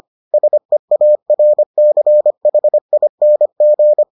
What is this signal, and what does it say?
Morse code, heard as a keyed tone, about 25 wpm, 600 Hz. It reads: SEARCHING